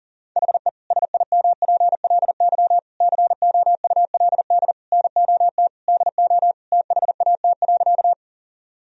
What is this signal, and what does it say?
Morse code, heard as a keyed tone, about 40 wpm, 700 Hz. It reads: HE SIMPLY COULD NOT DO THAT.
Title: Sentences from Top 1000 Words - Review Part 4 - 40wpm